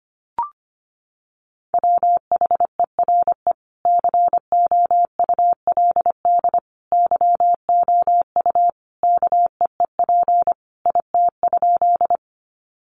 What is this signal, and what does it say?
Morse code, heard as a keyed tone, about 25 wpm, 700 Hz. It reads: WHERE COULD YOU KEEP IT?
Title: Sentences from Top 300 Words 25wpm